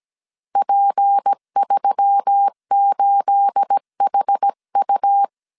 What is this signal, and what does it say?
Morse code, heard as a keyed tone, about 17 wpm, 800 Hz. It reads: P38HU
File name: CW / DET / 1 9 1500C S C 16wpm 2909